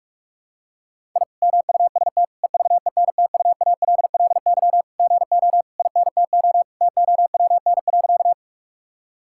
Morse code, heard as a keyed tone, about 41 wpm, 700 Hz. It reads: I MUST EVENTUALLY GO INTO TOWN.
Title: Sentences from Top 1000 Words - Review Part 6 - 45wpm